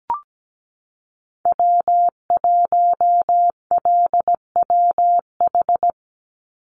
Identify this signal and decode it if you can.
Morse code, heard as a keyed tone, about 17 wpm, 700 Hz. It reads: W1LWH
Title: Call Signs Repeated 2x - 17wpm